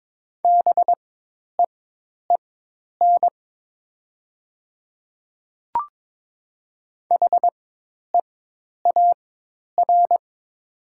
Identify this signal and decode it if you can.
Morse code, heard as a keyed tone, about 22 wpm, spaced as for 10 wpm, 700 Hz. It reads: BEEN HEAR
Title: 4x Character Spacing - Top 500 Words - 22wpm